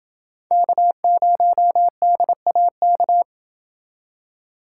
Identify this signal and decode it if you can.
Morse code, heard as a keyed tone, about 28 wpm, 700 Hz. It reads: K0DAK